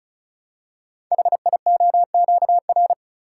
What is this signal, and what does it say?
Morse code, heard as a keyed tone, about 36 wpm, 700 Hz. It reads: HIOQR
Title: Code Groups - 5 Letter:Numbers - 35wpm